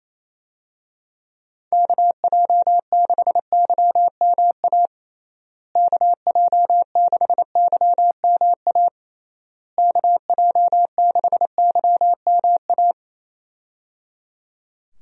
Morse code, heard as a keyed tone, about 28 wpm, 700 Hz. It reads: KJ6YMA KJ6YMA KJ6YMA
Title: USA Call Signs - 6 Characters - Repeated 3x - 28wpm